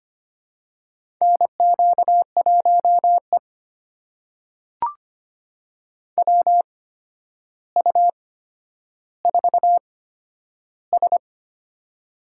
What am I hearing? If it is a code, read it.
Morse code, heard as a keyed tone, about 25 wpm, spaced as for 11 wpm, 700 Hz. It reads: NQ1E WU4S
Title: USA Call Signs - 4 Characters - 8x Character Spacing - 25wpm